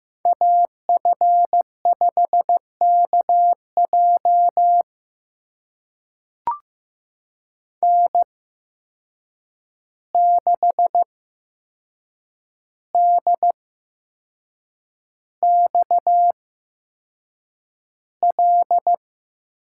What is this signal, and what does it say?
Morse code, heard as a keyed tone, about 15 wpm, spaced as for 6 wpm, 700 Hz. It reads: AF5KJ N6DXL
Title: USA Call Signs - 5 Characters - 8x Character Spacing - 15wpm